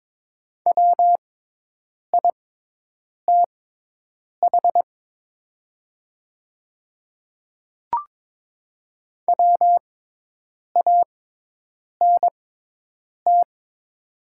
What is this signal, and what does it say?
Morse code, heard as a keyed tone, about 22 wpm, spaced as for 8 wpm, 700 Hz. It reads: WITH WANT
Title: 6x Character Spacing - Top 200 Words - 22wpm